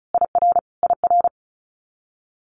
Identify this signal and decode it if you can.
Morse code, heard as a keyed tone, about 36 wpm, 700 Hz. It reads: IR IR
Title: Multi-letter Phonemes Repeated 3x 35wpm